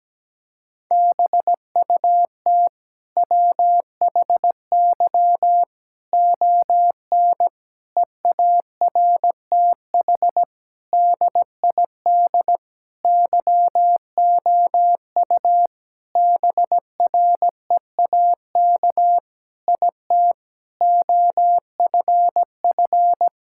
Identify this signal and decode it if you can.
Morse code, heard as a keyed tone, about 17 wpm, 700 Hz. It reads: BUT WHY ON EARTH DID YOU BREAK IT OFF
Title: Sentences from Top 1000 Words - Review Part 6 - 17wpm